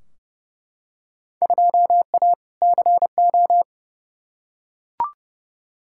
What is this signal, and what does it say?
Morse code, heard as a keyed tone, about 30 wpm, 700 Hz. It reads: 2A CO
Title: Field Day Exchanges - 30wpm